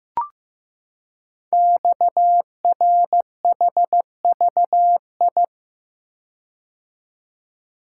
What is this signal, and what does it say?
Morse code, heard as a keyed tone, about 15 wpm, 700 Hz. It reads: XRHVI
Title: Code Groups - 5 Letters - 15wpm